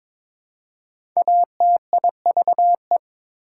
Morse code, heard as a keyed tone, about 22 wpm, 700 Hz. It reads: ATIVE
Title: Prefixes and Suffixes 22wpm